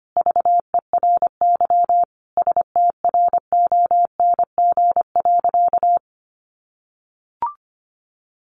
Morse code, heard as a keyed tone, about 25 wpm, 700 Hz. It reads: VERY STRONG.
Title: Sentences from Top 1300 Words 25wpm